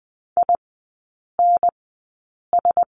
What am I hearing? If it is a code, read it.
Morse code, heard as a keyed tone, about 20 wpm, spaced as for 8 wpm, 700 Hz. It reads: INS
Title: 041 Lesson 8e - Number 1 - Three Characters 20wpm